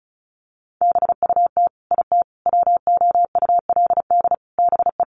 Morse code, heard as a keyed tone, about 34 wpm, 700 Hz. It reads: BUT IT WOULD BE
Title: Sentences from Top 1400 Words 35wpm